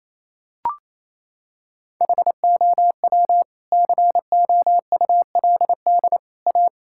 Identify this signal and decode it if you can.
Morse code, heard as a keyed tone, about 28 wpm, 700 Hz. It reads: HOW COULD A
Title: Sentences from Top 1000 Words - Review Part 6 - 28wpm